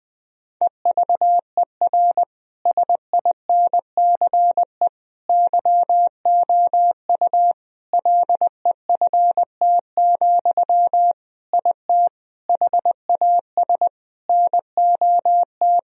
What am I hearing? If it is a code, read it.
Morse code, heard as a keyed tone, about 20 wpm, 700 Hz. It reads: EVER SINCE YOU LEFT, IT HAS NOT
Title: Sentences from Top 1000 Words - Review Part 1 - 20wpm